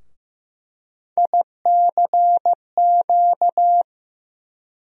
Morse code, heard as a keyed tone, about 15 wpm, 700 Hz. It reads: ICQ
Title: USA Call Sign Postfixes - 3 Letters 15wpm